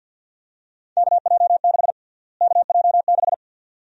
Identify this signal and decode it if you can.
Morse code, heard as a keyed tone, about 53 wpm, 700 Hz. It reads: KJ6 KJ6
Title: USA Call Sign Prefixes - Repeated 2x - 50wpm